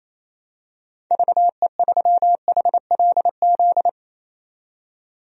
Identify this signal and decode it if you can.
Morse code, heard as a keyed tone, about 28 wpm, 700 Hz. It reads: VE3HLZ
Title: DXCC-100 Call Signs - 6 Characters - 28wpm